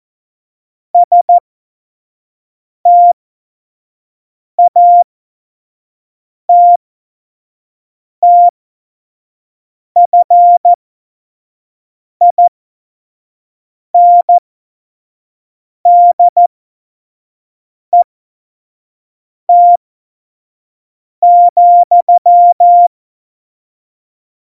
Morse code, heard as a keyed tone, about 14 wpm, spaced as for 5 wpm, 700 Hz. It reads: STATTFINDET,
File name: Morsekurs / Witze / Witz 15. September 2020, 25 F70